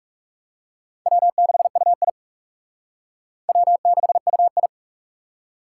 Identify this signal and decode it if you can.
Morse code, heard as a keyed tone, about 44 wpm, 700 Hz. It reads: W6UI W6UI